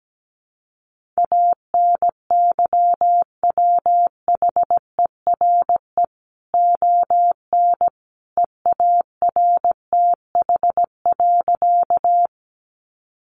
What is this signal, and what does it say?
Morse code, heard as a keyed tone, about 17 wpm, 700 Hz. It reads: ANYWHERE ON EARTH.
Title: Sentences from Top 1400 Words 17wpm